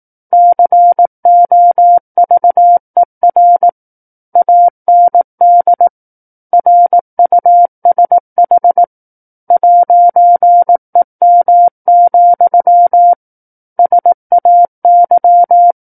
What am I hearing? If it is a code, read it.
Morse code, heard as a keyed tone, about 18 wpm, 700 Hz. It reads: COVER AND RUSH 'EM, SAY
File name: 18 WPM